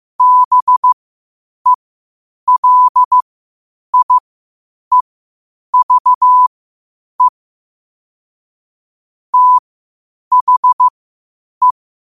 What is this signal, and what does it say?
Morse code, heard as a keyed tone, about 15 wpm, spaced as for 8 wpm, 1000 Hz. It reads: BELIEVE THE
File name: qotdfiles / 8wpm